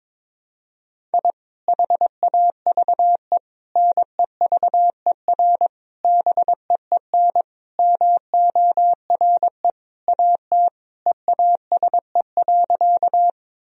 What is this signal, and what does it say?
Morse code, heard as a keyed tone, about 22 wpm, 700 Hz. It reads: I HAVE NEVER BEEN MORE AT EASE.
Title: Sentences from Top 1000 Words - Review Part 1 - 22wpm